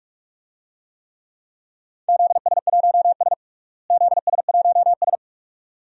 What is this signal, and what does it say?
Morse code, heard as a keyed tone, about 43 wpm, 700 Hz. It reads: ZS1S ZS1S
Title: DXCC-100 Call Signs - 4 Characters - Repeated 2x - 45wpm